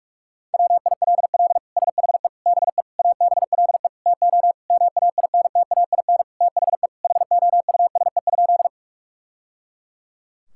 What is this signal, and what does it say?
Morse code, heard as a keyed tone, about 41 wpm, 700 Hz. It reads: WILL SHE BE ABLE TO MAINTAIN THE HOUSE?
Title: Sentences from Top 1000 Words - Review Part 5 - 45wpm